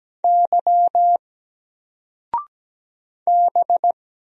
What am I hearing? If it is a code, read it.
Morse code, heard as a keyed tone, about 17 wpm, 700 Hz. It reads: Y B
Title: Single Letter-Number 17wpm